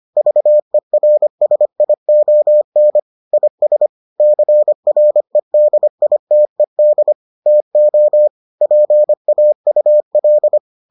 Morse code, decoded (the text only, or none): VERSION IS CREDITED TO PAUL